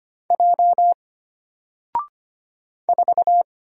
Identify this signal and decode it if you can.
Morse code, heard as a keyed tone, about 25 wpm, 700 Hz. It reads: J 4
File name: Single-Letter-Number-25wpm